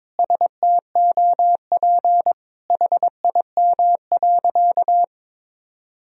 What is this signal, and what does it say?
Morse code, heard as a keyed tone, about 22 wpm, 700 Hz. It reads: STOP HIM.
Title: Sentences from Top 1000 Words - Review Part 4 - 22wpm